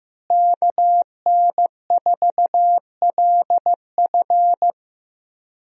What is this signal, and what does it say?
Morse code, heard as a keyed tone, about 15 wpm, 700 Hz. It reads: KN4LF
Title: USA Call Signs - 5 Characters - Repeated 2x - 15wpm